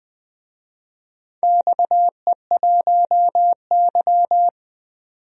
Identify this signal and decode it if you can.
Morse code, heard as a keyed tone, about 20 wpm, 700 Hz. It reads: XE1Y
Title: DXCC-100 Call Signs - 4 Characters - 20wpm